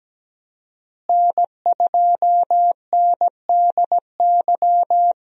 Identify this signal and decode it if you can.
Morse code, heard as a keyed tone, about 17 wpm, 700 Hz. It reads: N2NDY
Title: USA Call Signs - 5 Characters - Repeated 2x - 17wpm